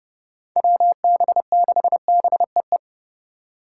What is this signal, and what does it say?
Morse code, heard as a keyed tone, about 32 wpm, 700 Hz. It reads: WB6BEE
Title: K1USN SST QSO Practice - 30wpm